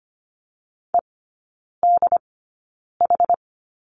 Morse code, heard as a keyed tone, about 25 wpm, spaced as for 9 wpm, 700 Hz. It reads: EDH
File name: Lesson 15e - Number 5 - Three Characters 25wpm